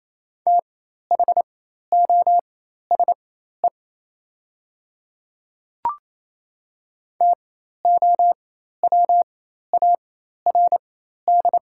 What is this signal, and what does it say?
Morse code, heard as a keyed tone, about 28 wpm, spaced as for 13 wpm, 700 Hz. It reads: THOSE TOWARD
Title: 4x Character Spacing - Top 500 Words - 28wpm